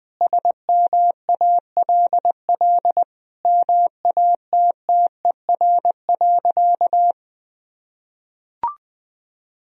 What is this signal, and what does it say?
Morse code, heard as a keyed tone, about 20 wpm, 700 Hz. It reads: SMALL MATTER.
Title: Sentences from Top 1000 Words - Review Part 1 - 20wpm